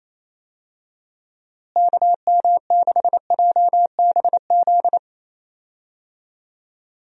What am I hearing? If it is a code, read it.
Morse code, heard as a keyed tone, about 27 wpm, 700 Hz. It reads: KM6JBZ
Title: USA Call Signs - 6 Characters - Repeated 2x - 28wpm